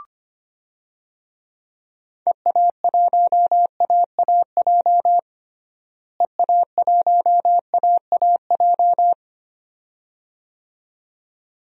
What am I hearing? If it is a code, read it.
Morse code, heard as a keyed tone, about 25 wpm, 700 Hz. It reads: EA1AAJ EA1AAJ
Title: DXCC-100 Call Signs - 6 Characters - Repeated 2x - 25wpm